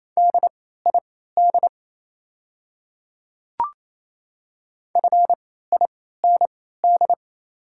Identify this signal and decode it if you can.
Morse code, heard as a keyed tone, about 28 wpm, spaced as for 16 wpm, 690 Hz. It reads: DID FIND
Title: 3x Character Spacing - Top 100 Words - 28wpm